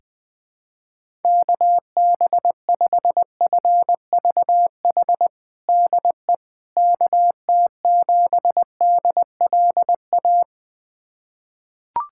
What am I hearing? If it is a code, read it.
Morse code, heard as a keyed tone, about 21 wpm, 700 Hz. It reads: KB5FVH DE KT7DLA
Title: USA Callsign DE Callsign 20wpm